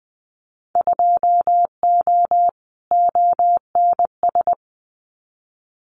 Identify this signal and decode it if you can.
Morse code, heard as a keyed tone, about 20 wpm, 700 Hz. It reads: 2O ONS
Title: Winter Field Day Exchanges Encore 20wpm